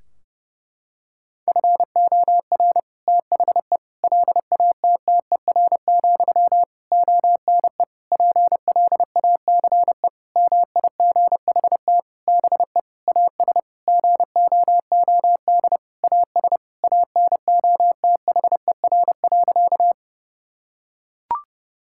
Morse code, heard as a keyed tone, about 30 wpm, 700 Hz. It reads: FOR THE LATTER, ONE PLACE MIGHT BE AS GOOD AS ANOTHER.
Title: Sentences from Top 1600 Words 30wpm